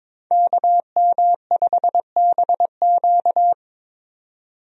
Morse code, heard as a keyed tone, about 21 wpm, 700 Hz. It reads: KM5BQ
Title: USA Call Signs - 5 Characters - 22wpm